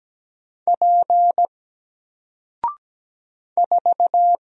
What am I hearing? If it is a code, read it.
Morse code, heard as a keyed tone, about 17 wpm, 700 Hz. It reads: P 4
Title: Single Letter-Number Alt 17wpm